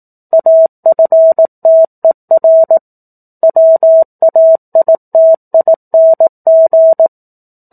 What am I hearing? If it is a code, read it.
Morse code, heard as a keyed tone, about 18 wpm, 650 Hz. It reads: AFTER WAITING
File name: 04-Chapter-04 18 WPM